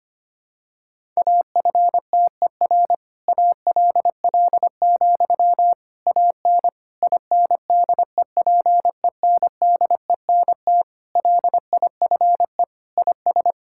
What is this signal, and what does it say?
Morse code, heard as a keyed tone, about 25 wpm, 700 Hz. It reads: AFTER ALL, AN INDEPENDENT LIFE IS
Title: Sentences from Top 1000 Words - Review Part 6 - 25wpm